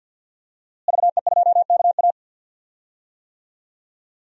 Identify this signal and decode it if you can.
Morse code, heard as a keyed tone, about 52 wpm, 700 Hz. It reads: VE2KA
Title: DXCC-100 Call Signs - 5 Characters - Repeated 2x - 50wpm